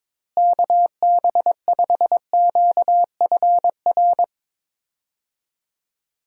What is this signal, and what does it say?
Morse code, heard as a keyed tone, about 22 wpm, 700 Hz. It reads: KB5QFR
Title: USA Call Signs - 6 Characters - Repeated 3x - 22wpm